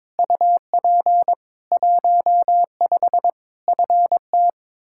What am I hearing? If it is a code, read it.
Morse code, heard as a keyed tone, about 22 wpm, 700 Hz. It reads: UP 15 FT
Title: Practice QSO - Exchange 2 - Info - 22wpm